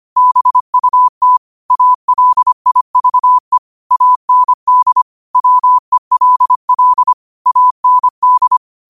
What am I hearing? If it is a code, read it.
Morse code, heard as a keyed tone, about 25 wpm, 1000 Hz. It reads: BUT ALIVE AND WELL AND
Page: Quotes for Thu, 14 Aug 2025 in Morse Code at 25 words per minute.